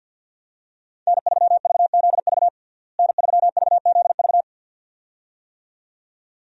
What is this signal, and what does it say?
Morse code, heard as a keyed tone, about 51 wpm, 700 Hz. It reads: N3VZV N3VZV
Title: USA Call Signs - 5 Characters - Repeated 2x - 50wpm